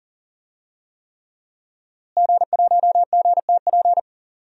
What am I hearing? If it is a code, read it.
Morse code, heard as a keyed tone, about 40 wpm, 700 Hz. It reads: G1GTP
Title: DXCC-100 Call Signs - 5 Characters - Repeated 3x - 40wpm